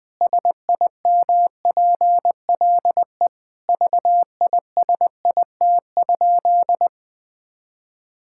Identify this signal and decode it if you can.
Morse code, heard as a keyed tone, about 20 wpm, 700 Hz. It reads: SIMPLE VISIT?
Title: Sentences from Top 1000 Words - Review Part 6 - 20wpm